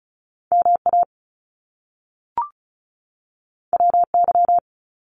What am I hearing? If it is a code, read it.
Morse code, heard as a keyed tone, about 36 wpm, 700 Hz. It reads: MA WY
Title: US States 35wpm